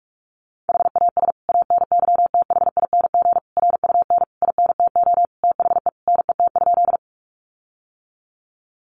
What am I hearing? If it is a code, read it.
Morse code, heard as a keyed tone, about 44 wpm, 700 Hz. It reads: HAS ANYTHING RUN INTO THE NET?
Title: Sentences from Top 1400 Words 45wpm